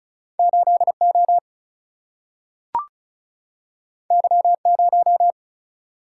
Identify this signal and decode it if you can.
Morse code, heard as a keyed tone, about 35 wpm, 700 Hz. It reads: 8O Y0
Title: Single-Letter Single-Number 35wpm